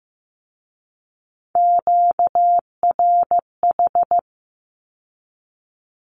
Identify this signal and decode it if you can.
Morse code, heard as a keyed tone, about 15 wpm, 700 Hz. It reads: QRH
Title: Q-Codes with Spoken Meaning - 15wpm